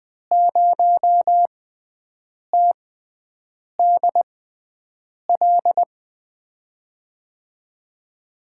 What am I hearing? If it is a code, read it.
Morse code, heard as a keyed tone, about 20 wpm, spaced as for 7 wpm, 700 Hz. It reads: 0TDL
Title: USA Call Signs - 6 Characters - 6x Character Spacing - 20wpm